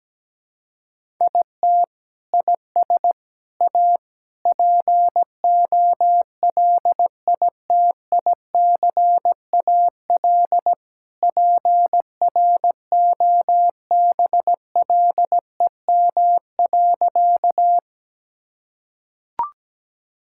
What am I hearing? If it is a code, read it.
Morse code, heard as a keyed tone, about 17 wpm, 700 Hz. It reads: IT IS A POLITICAL PROBLEM.
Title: Sentences from Top 1000 Words - Review Part 5 - 17wpm